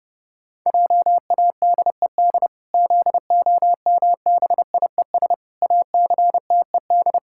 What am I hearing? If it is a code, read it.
Morse code, heard as a keyed tone, about 32 wpm, 700 Hz. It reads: JADED ZOMBIES ACTED
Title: Pangram Challenge v2 - 30wpm